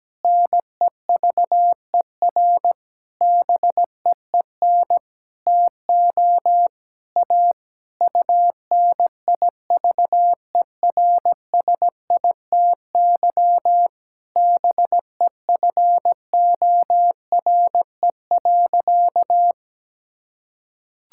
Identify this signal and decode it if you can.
Morse code, heard as a keyed tone, about 17 wpm, 700 Hz. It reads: NEVER BEEN TO A UNIVERSITY BEFORE.
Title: Sentences from Top 1000 Words - Review Part 5 - 17wpm